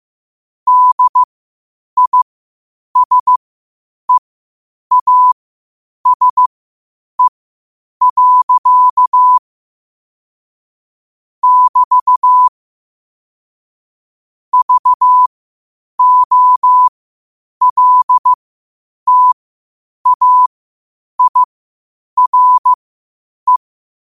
Morse code, heard as a keyed tone, about 15 wpm, spaced as for 8 wpm, 1000 Hz. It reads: DISEASE. <BT> VOLTAIRE
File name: qotdfiles / 8wpm